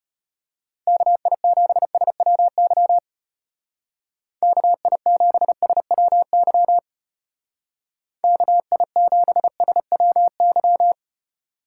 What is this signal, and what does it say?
Morse code, heard as a keyed tone, about 31 wpm, 700 Hz. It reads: KI7SWY KI7SWY KI7SWY